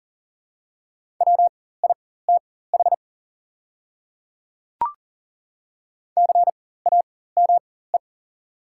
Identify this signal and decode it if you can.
Morse code, heard as a keyed tone, about 38 wpm, spaced as for 18 wpm, 700 Hz. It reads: WITH CAME
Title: 4x Character Spacing - Top 200 Words - 40wpm